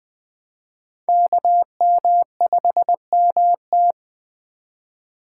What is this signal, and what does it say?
Morse code, heard as a keyed tone, about 20 wpm, 700 Hz. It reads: KM5MT